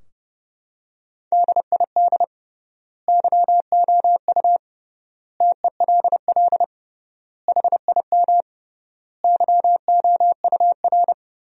Morse code, heard as a keyed tone, about 31 wpm, 700 Hz. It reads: DID YOU TELL HIM YOUR
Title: 3x Word-Spacing - Sentences from Top 400 Words - 30wpm